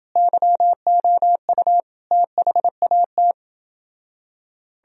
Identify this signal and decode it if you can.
Morse code, heard as a keyed tone, about 30 wpm, 700 Hz. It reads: YOU THAT